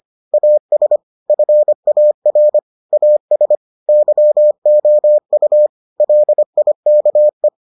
Morse code, heard as a keyed tone, about 25 wpm, 600 Hz. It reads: AS FAR AS YOU LIKE